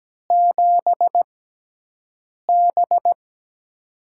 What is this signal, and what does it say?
Morse code, heard as a keyed tone, about 17 wpm, spaced as for 6 wpm, 700 Hz. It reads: 7B
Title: USA Call Signs - 6x Character Spacing - 17wpm